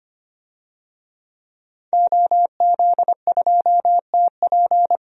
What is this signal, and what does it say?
Morse code, heard as a keyed tone, about 25 wpm, 700 Hz. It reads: OZ2TP
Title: DXCC-100 Call Signs - 5 Characters - 25wpm